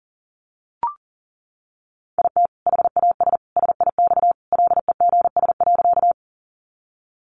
Morse code, heard as a keyed tone, about 40 wpm, 700 Hz. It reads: IT HAS SIX LEGS.
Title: Additional Easy Phrases 40wpm